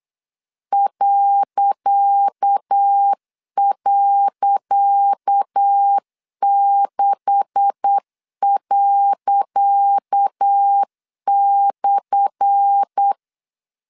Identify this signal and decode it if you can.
Morse code, heard as a keyed tone, about 8 wpm, 800 Hz. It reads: ..6./
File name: LEZIONI / 10 SIGNS / 10C DOT 8wpm 3649